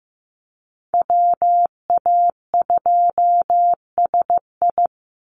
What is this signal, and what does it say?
Morse code, heard as a keyed tone, about 15 wpm, 700 Hz. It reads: WA2SI
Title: Call Signs Repeated 3x - 15wpm